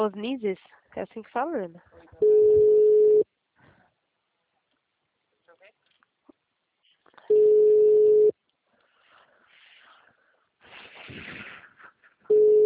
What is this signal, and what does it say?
Ringback tone (Europe, 425 Hz), with speech.